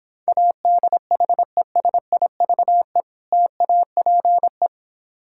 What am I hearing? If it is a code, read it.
Morse code, heard as a keyed tone, about 26 wpm, 700 Hz. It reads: ADHESIVE TAPE